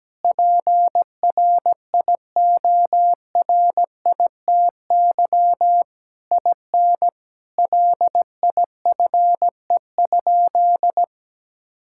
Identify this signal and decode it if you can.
Morse code, heard as a keyed tone, about 17 wpm, 700 Hz. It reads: PRIORITY IN LIFE?
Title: Sentences from Top 1600 Words 17wpm